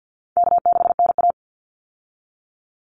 Morse code, heard as a keyed tone, about 50 wpm, 700 Hz. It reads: K6NA
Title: Easy Call Signs Repeated 2x - 50wpm